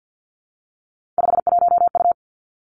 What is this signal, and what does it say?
Morse code, heard as a keyed tone, about 50 wpm, 700 Hz. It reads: 51U
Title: Three Challenging Letter-Number Combinations 50wpm